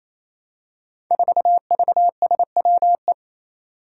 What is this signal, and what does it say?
Morse code, heard as a keyed tone, about 29 wpm, 700 Hz. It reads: 4VSWE
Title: Five Challenging Letter-Number Combinations 28wpm